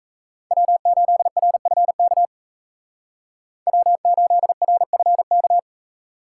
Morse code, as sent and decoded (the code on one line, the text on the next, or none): .-- ---.. .-. ..-. -.- / .-- ---.. .-. ..-. -.-
W8RFK W8RFK